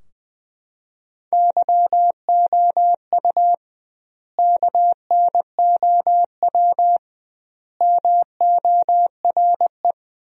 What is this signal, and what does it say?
Morse code, heard as a keyed tone, about 20 wpm, 700 Hz. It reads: YOU KNOW MORE
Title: 2x Word-Spacing - Sentences from Top 100 Words - 20wpm